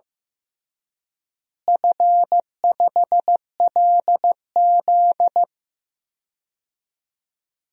Morse code, heard as a keyed tone, about 16 wpm, 700 Hz. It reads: F5LZ
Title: DXCC-100 Call Signs - 4 Characters - Repeated 2x - 15wpm